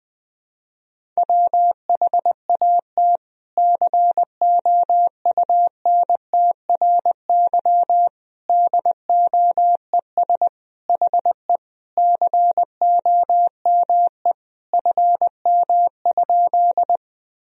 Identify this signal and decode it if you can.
Morse code, heard as a keyed tone, about 20 wpm, 700 Hz. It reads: WHAT COUNTRY DOES HE COME FM?
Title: Sentences from Top 300 Words and Abbreviations 20wpm